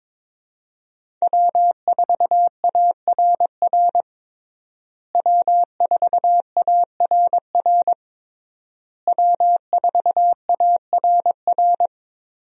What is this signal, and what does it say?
Morse code, heard as a keyed tone, about 22 wpm, 700 Hz. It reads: W4ARR W4ARR W4ARR
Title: USA Call Signs - 5 Characters - Repeated 3x - 22wpm